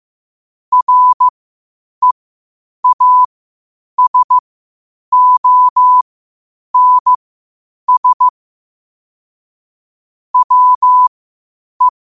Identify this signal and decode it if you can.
Morse code, heard as a keyed tone, about 15 wpm, spaced as for 8 wpm, 1000 Hz. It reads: REASONS WE
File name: qotdfiles / 8wpm